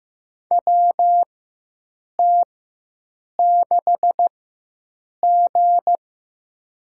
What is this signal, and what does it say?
Morse code, heard as a keyed tone, about 15 wpm, spaced as for 7 wpm, 700 Hz. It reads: WT6G
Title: USA Call Signs - 4 Characters - 4x Character Spacing - 15wpm